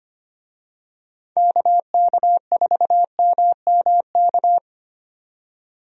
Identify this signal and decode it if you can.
Morse code, heard as a keyed tone, about 25 wpm, 700 Hz. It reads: KK4MMK
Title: USA Call Signs - 6 Characters - 25wpm